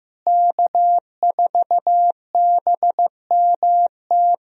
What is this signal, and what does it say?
Morse code, heard as a keyed tone, about 16 wpm, 700 Hz. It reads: K4BMT
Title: USA Call Signs - 5 Characters - Repeated 3x - 15wpm